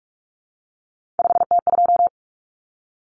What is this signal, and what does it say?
Morse code, heard as a keyed tone, about 44 wpm, 700 Hz. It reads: 5T2
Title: Common Cut Numbers - 3-Digit 45wpm